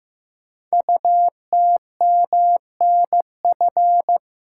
Morse code, heard as a keyed tone, about 15 wpm, 700 Hz. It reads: UTMNF
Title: Code Groups - 5 Letters - 15wpm